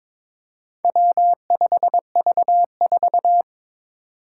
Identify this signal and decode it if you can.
Morse code, heard as a keyed tone, about 23 wpm, 700 Hz. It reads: W5V4
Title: Four Challenging Letter-Number Combinations 22wpm